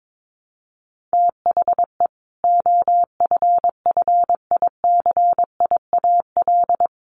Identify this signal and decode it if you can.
Morse code, heard as a keyed tone, about 22 wpm, 700 Hz. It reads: THE OFFICIAL
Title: A Sound Of Thunder 22wpm